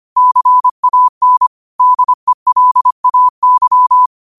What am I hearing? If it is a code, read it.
Morse code, heard as a keyed tone, about 25 wpm, 1000 Hz. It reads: CAN DELAY
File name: qotdfiles / 25wpm